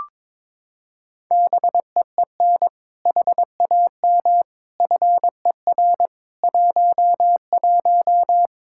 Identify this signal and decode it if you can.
Morse code, heard as a keyed tone, about 22 wpm, 700 Hz. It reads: BEEN HAM FER 11
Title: Practice QSO - Exchange 3 - More Info - 22wpm